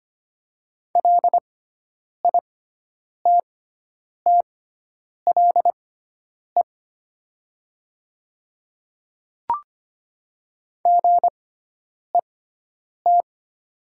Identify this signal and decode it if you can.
Morse code, heard as a keyed tone, about 25 wpm, spaced as for 9 wpm, 700 Hz. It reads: LITTLE GET
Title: 6x Character Spacing - Top 300 Words - 25wpm